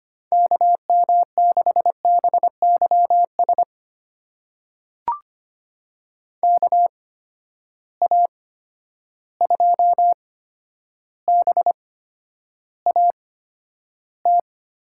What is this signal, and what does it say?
Morse code, heard as a keyed tone, about 25 wpm, spaced as for 11 wpm, 700 Hz. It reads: KM6BYS KA2BAT
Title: USA Call Signs - 6 Characters - 8x Character Spacing - 25wpm